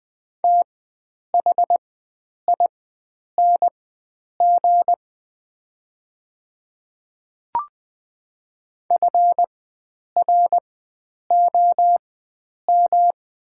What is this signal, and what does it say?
Morse code, heard as a keyed tone, about 20 wpm, spaced as for 9 wpm, 700 Hz. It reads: THING FROM
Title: 4x Character Spacing - Top 100 Words - 20wpm